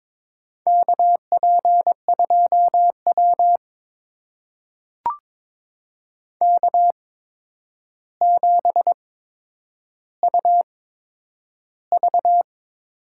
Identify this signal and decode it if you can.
Morse code, heard as a keyed tone, about 22 wpm, spaced as for 9 wpm, 700 Hz. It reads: KP2W K7UV
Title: USA Call Signs - 4 Characters - 8x Character Spacing - 22wpm